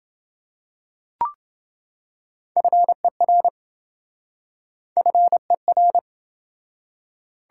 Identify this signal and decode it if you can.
Morse code, heard as a keyed tone, about 28 wpm, 700 Hz. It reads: FER FER